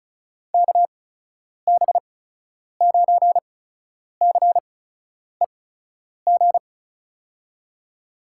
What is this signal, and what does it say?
Morse code, heard as a keyed tone, about 36 wpm, spaced as for 10 wpm, 700 Hz. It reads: KB9CEG
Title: USA Call Signs - 8x Character Spacing - 35wpm